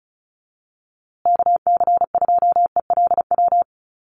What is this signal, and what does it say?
Morse code, heard as a keyed tone, about 34 wpm, 700 Hz. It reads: KC2ELW
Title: USA Call Signs Repeated 2x 35wpm